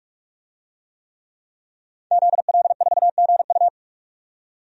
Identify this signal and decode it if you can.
Morse code, heard as a keyed tone, about 43 wpm, 700 Hz. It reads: ZL4GU
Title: DXCC-100 Call Signs - 5 Characters - 45wpm